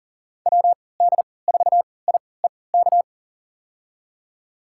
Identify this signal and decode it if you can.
Morse code, heard as a keyed tone, about 40 wpm, spaced as for 23 wpm, 700 Hz. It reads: WD4IEK
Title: USA Call Signs - 6 Characters - 3x Character Spacing - 40wpm